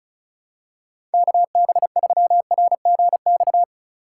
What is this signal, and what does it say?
Morse code, heard as a keyed tone, about 36 wpm, 700 Hz. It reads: KB3RGX
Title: USA Call Signs - 3x Character Spacing - 35wpm